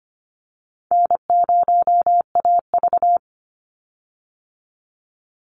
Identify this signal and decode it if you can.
Morse code, heard as a keyed tone, about 25 wpm, 700 Hz. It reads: N0AV
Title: Easy Call Signs Repeated 3x - 25wpm